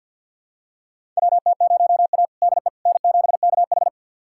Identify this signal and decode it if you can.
Morse code, heard as a keyed tone, about 48 wpm, 700 Hz. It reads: WT0A DE N7KH
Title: USA Callsign DE Callsign - 4 Characters - 50wpm